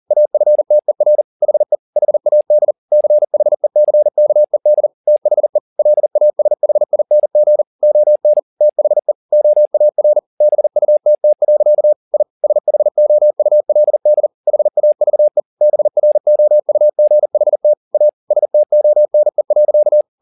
Morse code, heard as a keyed tone, about 40 wpm, 600 Hz. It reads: AFTER HE HAD CHECKED THE LASHING ON THE OAR BUTT. I SHOULD HAVE BROUGHT A STONE.
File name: the old man and the sea 2 40wpm